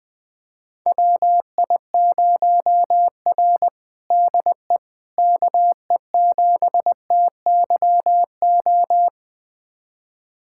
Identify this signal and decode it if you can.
Morse code, heard as a keyed tone, about 21 wpm, 700 Hz. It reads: WI0R DE KE7TYO
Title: USA Callsign DE Callsign 20wpm